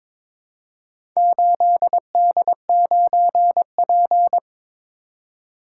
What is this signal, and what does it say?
Morse code, heard as a keyed tone, about 22 wpm, 700 Hz. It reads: 8D9P
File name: practice / 22wpm / Four Letter-Number Combinations - 22wpm